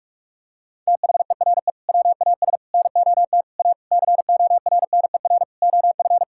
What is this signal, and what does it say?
Morse code, heard as a keyed tone, about 46 wpm, 700 Hz. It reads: THERE WAS NOT A CORNER OF
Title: Sentences from Top 1600 Words 45wpm